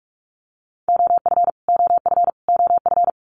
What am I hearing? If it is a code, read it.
Morse code, heard as a keyed tone, about 44 wpm, 700 Hz. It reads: OF OF OF
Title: Top 500 Words Repeated 45wpm